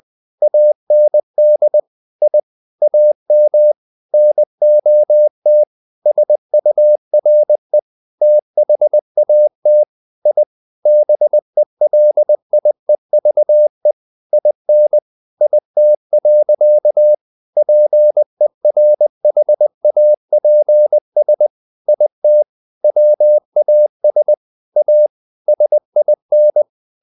Morse code, heard as a keyed tone, about 20 wpm, 600 Hz. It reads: AND I AM NOT SURE THAT I BELIEVE IN IT. PERHAPS IT WAS A SIN
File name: the old man and the sea part 2 20wpm